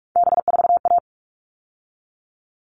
Morse code, heard as a keyed tone, about 44 wpm, 700 Hz. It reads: B4A
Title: All Cut Numbers - 3-Digit 45wpm